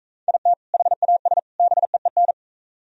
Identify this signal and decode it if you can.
Morse code, heard as a keyed tone, about 41 wpm, 700 Hz. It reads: IT HAS BEEN